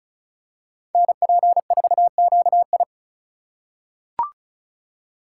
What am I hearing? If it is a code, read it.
Morse code, heard as a keyed tone, about 36 wpm, 700 Hz. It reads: NP4QI
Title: USA Call Signs - 5 Characters - 3x Character Spacing - 35wpm